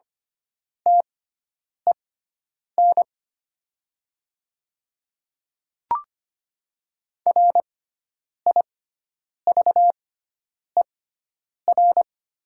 Morse code, heard as a keyed tone, about 25 wpm, spaced as for 9 wpm, 700 Hz. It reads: TEN RIVER
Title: 6x Character Spacing - Top 300 Words - 25wpm